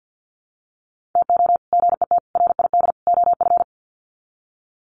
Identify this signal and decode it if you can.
Morse code, heard as a keyed tone, about 50 wpm, 700 Hz. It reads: TO GET RID OF
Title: Sets of 4 Words 50wpm